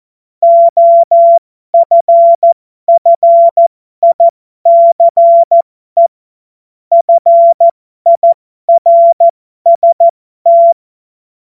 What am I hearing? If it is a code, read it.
Morse code, heard as a keyed tone, about 14 wpm, spaced as for 12 wpm, 680 Hz. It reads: OFFICE FIRST
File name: Morsekurs / Witze